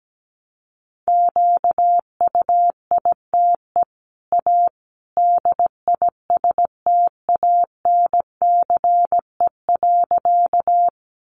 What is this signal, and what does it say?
Morse code, heard as a keyed tone, about 17 wpm, 700 Hz. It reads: QUITE A DISTANCE.
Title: Sentences from Top 1200 Words 17wpm